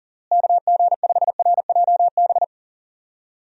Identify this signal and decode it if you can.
Morse code, heard as a keyed tone, about 40 wpm, 700 Hz. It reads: KG5RJB
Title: USA Call Signs - 6 Characters - Repeated 3x - 40wpm